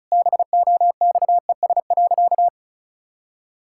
Morse code, heard as a keyed tone, about 36 wpm, 700 Hz. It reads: BOXES.
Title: Pangram Challenge v2 - 35wpm